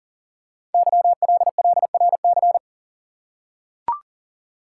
Morse code, heard as a keyed tone, about 38 wpm, 690 Hz. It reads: YLLRC
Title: Code Groups - 5 Letters - 40wpm